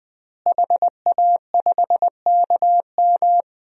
Morse code, heard as a keyed tone, about 20 wpm, 700 Hz. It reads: HA5KM
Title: DXCC-100 Call Signs - 5 Characters - Repeated 2x - 20wpm